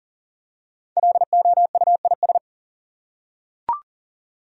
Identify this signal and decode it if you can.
Morse code, heard as a keyed tone, about 40 wpm, 700 Hz. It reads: LOUIS
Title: USA Common Names - Max 5 Letters - 40wpm